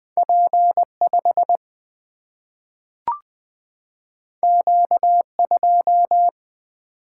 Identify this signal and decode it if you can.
Morse code, heard as a keyed tone, about 20 wpm, 700 Hz. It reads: P5 Q2
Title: Single-Letter Single-Number 20wpm